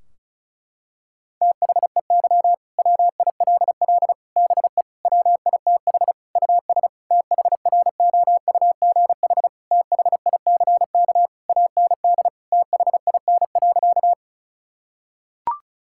Morse code, heard as a keyed tone, about 36 wpm, 700 Hz. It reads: THEY WILL BE WITH US THROUGH THICK AND THIN.
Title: Sentences from Top 1600 Words 35wpm